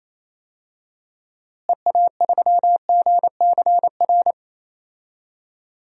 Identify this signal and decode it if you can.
Morse code, heard as a keyed tone, about 28 wpm, 700 Hz. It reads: EA3GCR
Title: DXCC-100 Call Signs - 6 Characters - Repeated 3x - 28wpm